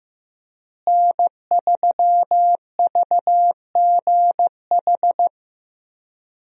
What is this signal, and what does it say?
Morse code, heard as a keyed tone, about 15 wpm, 700 Hz. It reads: N3VGH
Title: USA Call Signs - 5 Characters - Repeated 3x - 15wpm